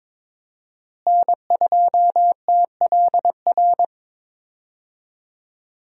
Morse code, heard as a keyed tone, about 22 wpm, 700 Hz. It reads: N2TLR
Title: USA Call Signs - 5 Characters - Repeated 2x - 22wpm